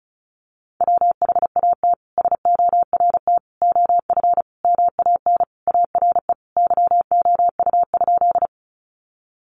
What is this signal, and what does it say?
Morse code, heard as a keyed tone, about 34 wpm, 700 Hz. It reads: WHAT SORT OF MAN ARE YOU?
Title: Sentences from Top 1300 Words 35wpm